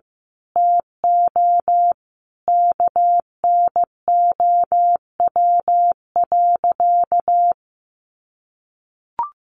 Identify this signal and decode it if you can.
Morse code, heard as a keyed tone, about 15 wpm, 700 Hz. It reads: TO KNOW.
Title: Sentences from Top 1200 Words 15wpm